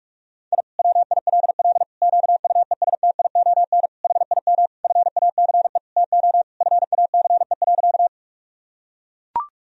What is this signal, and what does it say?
Morse code, heard as a keyed tone, about 41 wpm, 700 Hz. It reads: I WILL QUESTION HIM FACE TO FACE.
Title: Sentences from Top 1000 Words - Review Part 2 - 45wpm